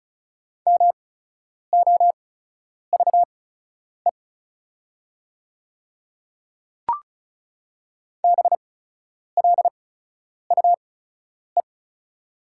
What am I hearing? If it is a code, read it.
Morse code, heard as a keyed tone, about 34 wpm, spaced as for 10 wpm, 690 Hz. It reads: MOVE BLUE
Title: 8x Character Spacing - Top 500 Words - 35wpm